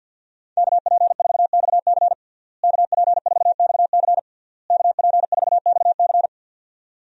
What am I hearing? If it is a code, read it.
Morse code, heard as a keyed tone, about 53 wpm, 700 Hz. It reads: KP4XC KP4XC KP4XC
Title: DXCC-100 Call Signs - 5 Characters - Repeated 3x - 50wpm